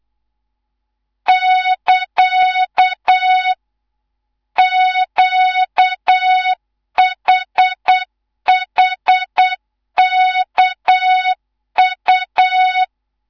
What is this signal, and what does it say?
Morse code, heard as a keyed tone, about 8 wpm, 3700 Hz, with noise. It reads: <KA> QHHKU
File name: Morze-HA2MN / hang / 19 GYAKORLAT